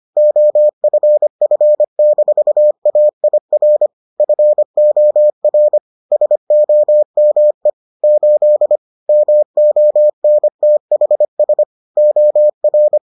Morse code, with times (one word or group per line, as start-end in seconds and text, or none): 0.2-3.9 OFF-AIR
4.2-5.8 FOR
6.1-7.7 SOME
8.0-8.8 8
9.1-11.6 MONTHS
12.0-13.0 OR